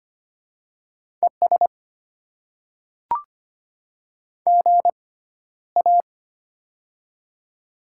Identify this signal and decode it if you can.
Morse code, heard as a keyed tone, about 25 wpm, spaced as for 13 wpm, 690 Hz. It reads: ES GA
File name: Top 20 QSO Elements - 6x Character Spacing - 25wpm